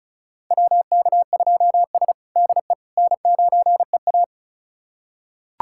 Morse code, heard as a keyed tone, about 36 wpm, 700 Hz. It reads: WK2S DE N9EA
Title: USA Callsign DE Callsign - 4 Characters - 35wpm